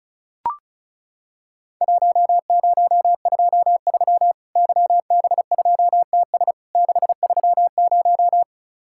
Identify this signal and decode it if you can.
Morse code, heard as a keyed tone, about 36 wpm, 700 Hz. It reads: 1023 YB2TS 630
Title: Worked All Europe DX Contest - QTC Exchange - 35wpm